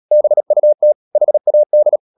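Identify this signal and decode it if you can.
Morse code, heard as a keyed tone, about 37 wpm, 600 Hz. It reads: BUT HAD
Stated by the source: Morse code practice at 35–39 wpm sending English text of Ken Follett's novel Never, from CW med Gnister.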